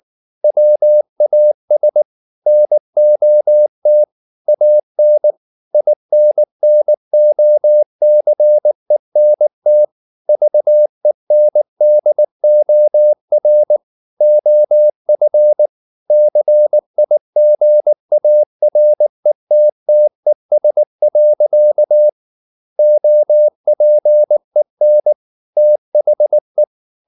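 Morse code, heard as a keyed tone, about 19 wpm, 600 Hz. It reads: WAS NOT AN INNOCENT VENDOR OF CIGARETTES. OPEN THE
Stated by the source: Morse code at 19 wpm from CW med Gnister, reading Ken Follett's novel Never in English.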